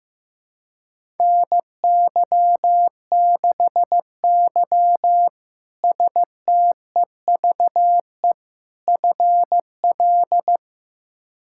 Morse code, heard as a keyed tone, about 15 wpm, 700 Hz. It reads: NY6Y STEVE FL
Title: K1USN SST QSO Practice - 15wpm